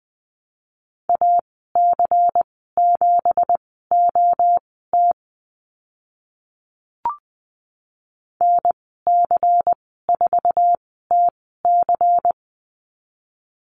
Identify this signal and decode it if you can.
Morse code, heard as a keyed tone, about 20 wpm, spaced as for 14 wpm, 700 Hz. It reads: AC7OT NC4TC
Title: 167 Lesson 27g - Number 7 - Call Signs 20wpm